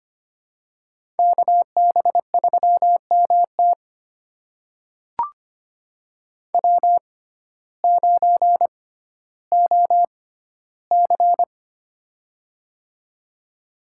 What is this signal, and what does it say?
Morse code, heard as a keyed tone, about 25 wpm, spaced as for 14 wpm, 700 Hz. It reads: KB3MT W9OC
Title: USA Call Signs - 6x Character Spacing - 25wpm